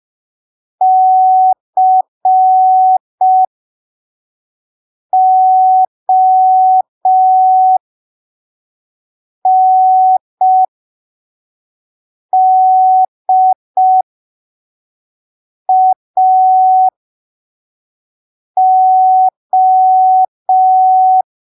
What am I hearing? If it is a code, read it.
Morse code, heard as a keyed tone, about 5 wpm, spaced as for 3 wpm, 750 Hz. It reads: CONDAO
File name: arquivos / texto / texto 02 5ppm